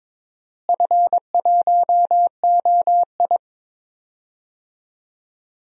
Morse code, heard as a keyed tone, about 22 wpm, 700 Hz. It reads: F1OI
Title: DXCC-100 Call Signs - 4 Characters - Repeated 2x - 22wpm